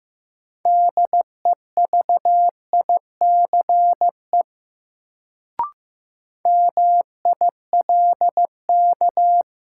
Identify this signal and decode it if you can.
Morse code, heard as a keyed tone, about 15 wpm, 700 Hz. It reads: DEVICE MILK
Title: Sentences from Top 1500 Words - New Word Drill - 15wpm